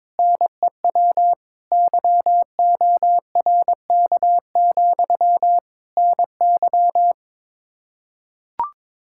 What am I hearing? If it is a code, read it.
Morse code, heard as a keyed tone, about 22 wpm, 700 Hz. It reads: NEW YORK, NY
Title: K1USN SST QSO Practice 22wpm